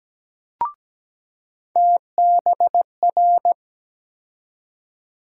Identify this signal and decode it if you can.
Morse code, heard as a keyed tone, about 17 wpm, 700 Hz. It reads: TBR
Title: USA Call Sign Postfixes - 3 Letters 17wpm